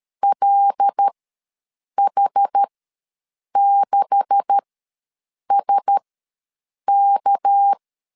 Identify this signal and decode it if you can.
Morse code, heard as a keyed tone, about 13 wpm, spaced as for 7 wpm, 800 Hz. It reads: LH6SK
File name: LEZIONI / 10 SIGNS / 10A SLASH ws 12wpm 3443